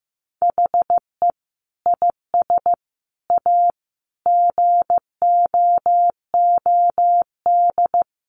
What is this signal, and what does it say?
Morse code, heard as a keyed tone, about 15 wpm, 700 Hz. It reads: HE IS A GOOD
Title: Sentences from Top 1100 Words 15wpm